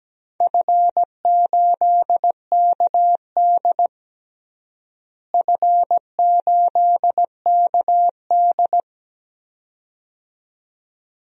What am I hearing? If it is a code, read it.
Morse code, heard as a keyed tone, about 17 wpm, 700 Hz. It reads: F8KD F8KD
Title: DXCC-100 Call Signs - 4 Characters - Repeated 2x - 17wpm